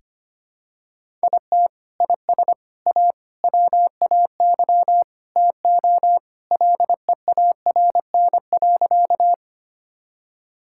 Morse code, heard as a keyed tone, about 25 wpm, 700 Hz. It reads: IT IS A WAY TO LEARN.
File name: practice / 25wpm / Sentences with 3-Word Phrases - 25wpm